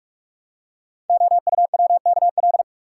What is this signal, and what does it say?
Morse code, heard as a keyed tone, about 44 wpm, 700 Hz. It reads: OUWKL
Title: Code Groups - 5 Letters - 45wpm